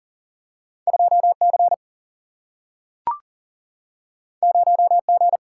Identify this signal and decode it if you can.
Morse code, heard as a keyed tone, about 39 wpm, 700 Hz. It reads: 2C 0G
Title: Single-Letter Single-Number 40wpm